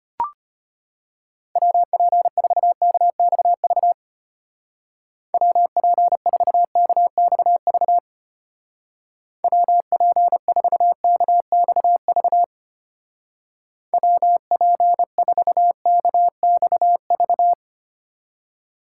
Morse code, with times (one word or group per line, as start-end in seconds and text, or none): 1.6-3.9 WP4KXV
5.3-8.0 WP4KXV
9.4-12.4 WP4KXV
13.9-17.5 WP4KXV